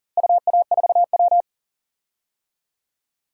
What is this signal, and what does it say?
Morse code, heard as a keyed tone, about 37 wpm, 700 Hz. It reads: UA4W
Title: DXCC-100 Call Signs - 4 Characters - Repeated 2x - 40wpm